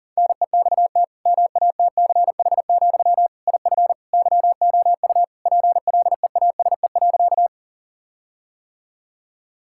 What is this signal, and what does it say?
Morse code, heard as a keyed tone, about 40 wpm, 700 Hz. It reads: NEXT MATCH, IF YOU PLEASE.
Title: Sentences from Top 1600 Words 40wpm